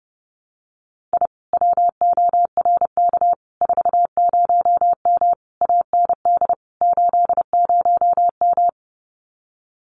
Morse code, heard as a keyed tone, about 30 wpm, 700 Hz. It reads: I WORK 40M AND 80M
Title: Common QSO Phrases Encore 30wpm